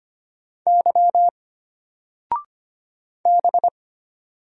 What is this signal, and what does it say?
Morse code, heard as a keyed tone, about 25 wpm, 700 Hz. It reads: Y B
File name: Single-Letter-Number-25wpm